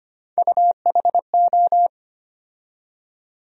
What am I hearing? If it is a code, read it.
Morse code, heard as a keyed tone, about 25 wpm, 700 Hz. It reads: UHO
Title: USA Call Sign Postfixes - 3 Letters 25wpm